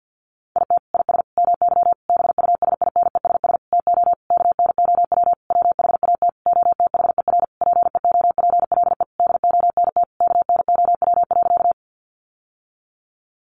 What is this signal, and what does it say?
Morse code, heard as a keyed tone, about 50 wpm, 700 Hz. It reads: IT IS MY BUSINESS TO KNOW WHAT OTHER PEOPLE DONT KNOW.
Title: Sentences from Top 600 Words 50wpm